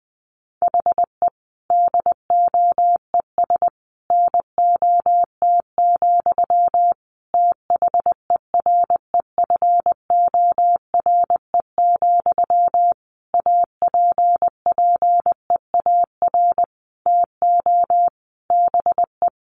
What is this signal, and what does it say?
Morse code, heard as a keyed tone, about 20 wpm, 700 Hz. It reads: HE DOES NOT, THEREFORE, APPEAR TO BE
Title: Sentences from Top 1200 Words 20wpm